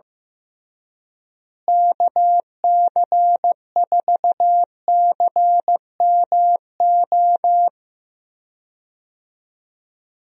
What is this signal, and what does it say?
Morse code, heard as a keyed tone, about 15 wpm, 700 Hz. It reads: KC4CMO
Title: USA Call Signs - 6 Characters - Repeated 3x - 15wpm